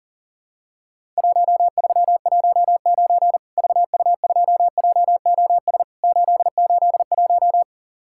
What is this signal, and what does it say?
Morse code, heard as a keyed tone, about 40 wpm, 700 Hz. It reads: 1319 VU2JOS 881
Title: Worked All Europe DX Contest - QTC Exchange - 40wpm